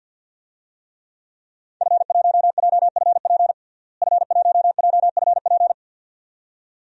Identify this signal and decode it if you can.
Morse code, heard as a keyed tone, about 47 wpm, 700 Hz. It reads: F1JFP F1JFP
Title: DXCC-100 Call Signs - 5 Characters - Repeated 2x - 50wpm